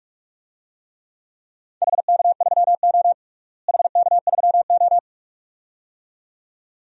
Morse code, heard as a keyed tone, about 48 wpm, 700 Hz. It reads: HK3O HK3O
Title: DXCC-100 Call Signs - 4 Characters - Repeated 2x - 45wpm